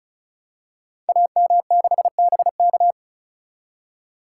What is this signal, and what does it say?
Morse code, heard as a keyed tone, about 36 wpm, 700 Hz. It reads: AM6BK
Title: Code Groups - 5 Letter:Numbers - 35wpm